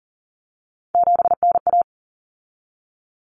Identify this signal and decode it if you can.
Morse code, heard as a keyed tone, about 40 wpm, 700 Hz. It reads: 7NA
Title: Common Cut Numbers - 3-Digit 40wpm